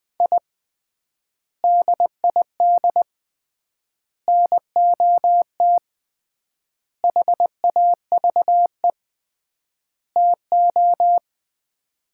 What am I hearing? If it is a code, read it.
Morse code, heard as a keyed tone, about 20 wpm, 700 Hz. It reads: I DID NOT HAVE TO
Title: 3x Word-Spacing - Sets of 5 Words - 20wpm